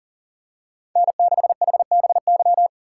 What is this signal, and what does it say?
Morse code, heard as a keyed tone, about 39 wpm, 700 Hz. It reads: N6HBY
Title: USA Call Signs - 5 Characters - 2x Character Spacing - 40wpm